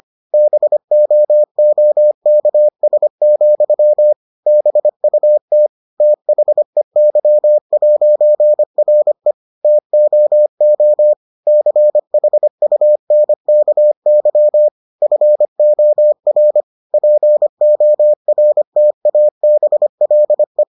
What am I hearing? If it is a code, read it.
Morse code, heard as a keyed tone, about 25 wpm, 600 Hz. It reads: BOOKS, BUT THEY'RE TOO CHUNKY FOR PORTABLE